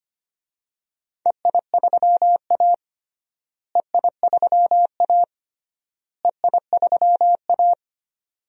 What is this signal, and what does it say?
Morse code, heard as a keyed tone, about 25 wpm, 700 Hz. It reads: EI3A EI3A EI3A
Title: DXCC-100 Call Signs - 4 Characters - Repeated 3x - 25wpm